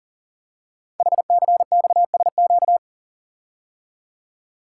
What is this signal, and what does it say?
Morse code, heard as a keyed tone, about 39 wpm, 700 Hz. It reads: HCXSQ
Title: Code Groups - 5 Letters - 40wpm